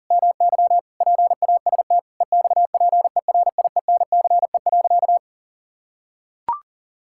Morse code, heard as a keyed tone, about 40 wpm, 700 Hz. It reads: MY PAST EXPERIENCE.
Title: Sentences from Top 1000 Words - Review Part 6 - 40wpm